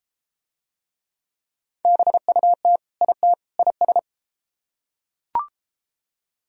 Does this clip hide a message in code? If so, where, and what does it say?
Yes: Morse code from 1.8 to 4.0 s, heard as a keyed tone, about 32 wpm, 700 Hz. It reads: BUT IT IS